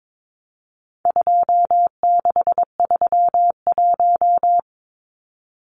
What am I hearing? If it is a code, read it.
Morse code, heard as a keyed tone, about 22 wpm, 700 Hz. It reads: 2631
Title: Numbers - 4-Digit 22wpm